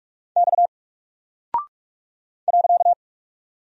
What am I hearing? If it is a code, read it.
Morse code, heard as a keyed tone, about 44 wpm, 700 Hz. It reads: X .
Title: Single Character (Letter:Number:Punctuation:Prosign) Alt 45wpm